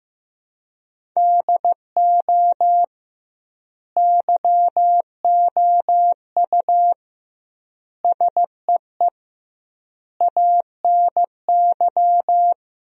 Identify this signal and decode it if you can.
Morse code, heard as a keyed tone, about 15 wpm, 700 Hz. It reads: DO YOU SEE ANY
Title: 2x Word-Spacing - Sentences from Top 500 Words - 15wpm